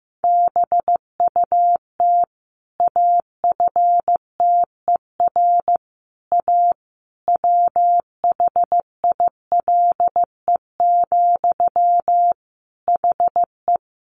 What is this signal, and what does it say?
Morse code, heard as a keyed tone, about 15 wpm, 700 Hz. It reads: BUT AFTER A WHILE, HE
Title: Sentences from Top 1500 Words 15wpm